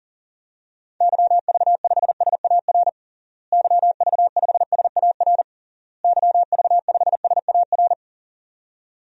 Morse code, heard as a keyed tone, about 40 wpm, 700 Hz. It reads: YV5SAR YV5SAR YV5SAR
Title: DXCC-100 Call Signs - 6 Characters - Repeated 3x - 40wpm